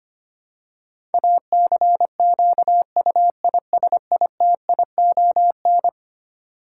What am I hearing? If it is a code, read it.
Morse code, heard as a keyed tone, about 25 wpm, 700 Hz. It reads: ACQUISITION
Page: morse-code-ninja-practice-25wpm